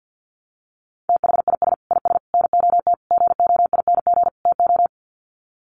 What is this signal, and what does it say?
Morse code, heard as a keyed tone, about 50 wpm, 700 Hz. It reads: THIS IS NOT GOING TO
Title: Sets of 5 Words Encore2 50wpm